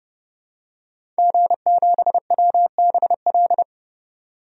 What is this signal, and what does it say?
Morse code, heard as a keyed tone, about 29 wpm, 700 Hz. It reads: G7WBL
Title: DXCC-100 Call Signs - 5 Characters - 30wpm